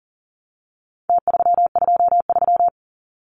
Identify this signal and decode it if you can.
Morse code, heard as a keyed tone, about 40 wpm, 700 Hz. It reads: T323
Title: Numbers - Contest Style - 4-Digit 40wpm